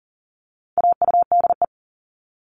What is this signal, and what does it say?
Morse code, heard as a keyed tone, about 40 wpm, 700 Hz. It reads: AUDE
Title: All Cut Numbers - 4-Digit 40wpm